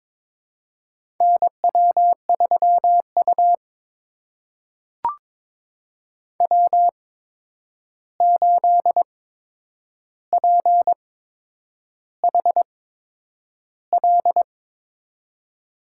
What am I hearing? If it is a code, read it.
Morse code, heard as a keyed tone, about 23 wpm, spaced as for 9 wpm, 700 Hz. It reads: NW3U W8PHL
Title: USA Call Signs - 8x Character Spacing - 22wpm